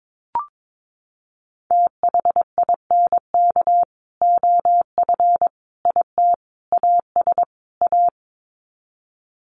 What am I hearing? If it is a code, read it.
Morse code, heard as a keyed tone, about 22 wpm, 700 Hz. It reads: THINK OF IT AS A
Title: Sets of 5 Words Encore2 22wpm